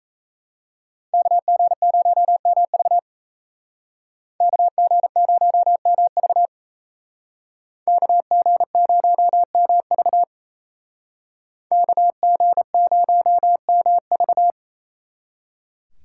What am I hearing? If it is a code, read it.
Morse code, heard as a keyed tone, about 36 wpm, 700 Hz. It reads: KG0MV KG0MV KG0MV KG0MV